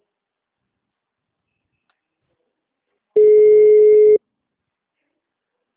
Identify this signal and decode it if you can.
Ringback tone (Europe, 425 Hz).